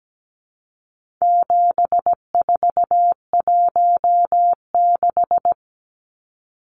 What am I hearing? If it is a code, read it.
Morse code, heard as a keyed tone, about 17 wpm, 700 Hz. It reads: 7416
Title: Numbers - 4-Digit 17wpm